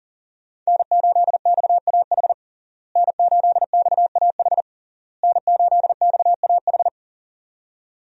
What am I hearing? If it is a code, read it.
Morse code, heard as a keyed tone, about 40 wpm, 700 Hz. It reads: N8XAH N8XAH N8XAH
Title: USA Call Signs - 5 Characters - Repeated 3x - 40wpm